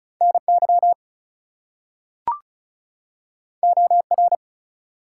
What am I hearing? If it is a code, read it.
Morse code, heard as a keyed tone, about 36 wpm, 700 Hz. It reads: NY OR
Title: US States and Canadian Provinces 35wpm